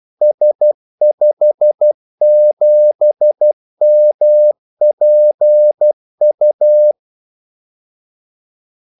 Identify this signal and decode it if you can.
Morse code, heard as a keyed tone, about 12 wpm, 600 Hz. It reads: S57MPU
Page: 2 ekstra ordmellemrum mellem kaldesignalerne